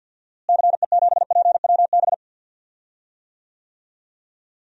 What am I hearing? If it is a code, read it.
Morse code, heard as a keyed tone, about 53 wpm, 700 Hz. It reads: CE7PWB
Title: DXCC-100 Call Signs - 6 Characters - Repeated 2x - 50wpm